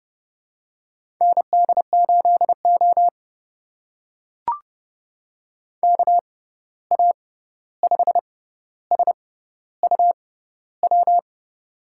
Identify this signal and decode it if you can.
Morse code, heard as a keyed tone, about 32 wpm, spaced as for 14 wpm, 700 Hz. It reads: ND8O KA5SUW
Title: USA Call Signs - 6x Character Spacing - 30wpm